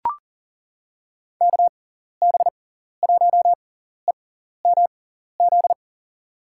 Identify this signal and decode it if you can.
Morse code, heard as a keyed tone, about 40 wpm, spaced as for 14 wpm, 700 Hz. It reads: KB1EMZ
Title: USA Call Signs - 6 Characters - 6x Character Spacing - 40wpm